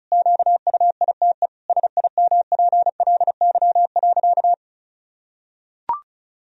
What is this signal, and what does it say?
Morse code, heard as a keyed tone, about 34 wpm, 700 Hz. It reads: QUITE SIMPLY.
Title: Sentences from Top 1000 Words - Review Part 4 - 35wpm